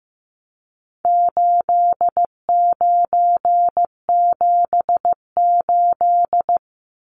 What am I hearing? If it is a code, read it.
Morse code, heard as a keyed tone, about 15 wpm, 700 Hz. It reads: 8978
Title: Numbers - 4-Digit 15wpm